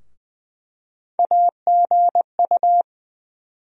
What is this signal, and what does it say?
Morse code, heard as a keyed tone, about 20 wpm, 700 Hz. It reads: AGU
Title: USA Call Sign Postfixes - 3 Letters 20wpm